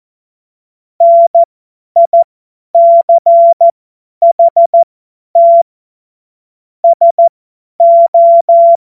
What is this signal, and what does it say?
Morse code, heard as a keyed tone, about 14 wpm, spaced as for 10 wpm, 680 Hz. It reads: NICHT SO
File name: Morsekurs / Witze / Witz 06. Januar 2026 50 F70